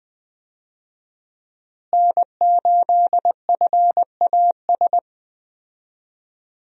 Morse code, heard as a keyed tone, about 20 wpm, 700 Hz. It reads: N8FAS
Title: USA Call Signs - 5 Characters - Repeated 2x - 20wpm